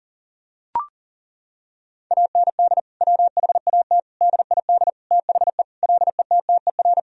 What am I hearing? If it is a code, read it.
Morse code, heard as a keyed tone, about 40 wpm, 700 Hz. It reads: AND WHAT DID THE LETTER
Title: Sentences from Top 1000 Words - Review Part 2 - 40wpm